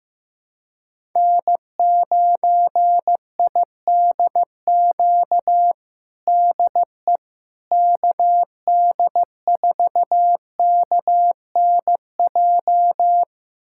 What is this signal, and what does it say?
Morse code, heard as a keyed tone, about 15 wpm, 700 Hz. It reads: N9IDQ DE KD4KNJ
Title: USA Callsign DE Callsign 15wpm